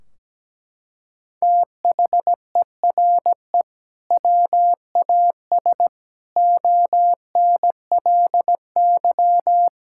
Morse code, heard as a keyed tone, about 17 wpm, 700 Hz. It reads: THERE WAS ONLY
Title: Sentences from Top 1600 Words 17wpm